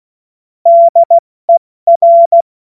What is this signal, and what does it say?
Morse code, heard as a keyed tone, about 16 wpm, spaced as for 14 wpm, 680 Hz. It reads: DER
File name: Witz-30-September-2025-70-F80